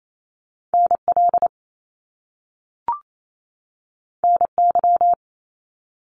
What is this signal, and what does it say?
Morse code, heard as a keyed tone, about 28 wpm, 700 Hz. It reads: NL NY
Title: Top Two Letter Combinations in Words 28wpm